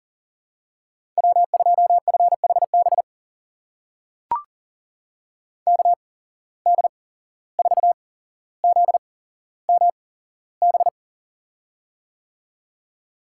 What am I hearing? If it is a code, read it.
Morse code, heard as a keyed tone, about 40 wpm, spaced as for 16 wpm, 700 Hz. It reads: W2FHB KD4ZMB
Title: USA Call Signs - 8x Character Spacing - 40wpm